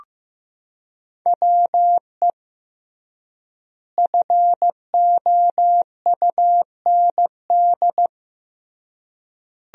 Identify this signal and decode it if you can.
Morse code, heard as a keyed tone, about 15 wpm, 700 Hz. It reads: WE FOUND
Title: 3x Word-Spacing - Sentences from Top 300 Words - 15wpm